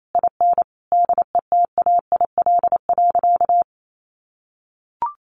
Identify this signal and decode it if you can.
Morse code, heard as a keyed tone, about 28 wpm, 700 Hz. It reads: IN DETAIL.
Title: Sentences from Top 1500 Words 28wpm